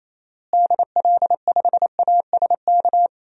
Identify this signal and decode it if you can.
Morse code, heard as a keyed tone, about 28 wpm, 700 Hz. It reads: DL5ASK
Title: DXCC-100 Call Signs - 6 Characters - Repeated 3x - 28wpm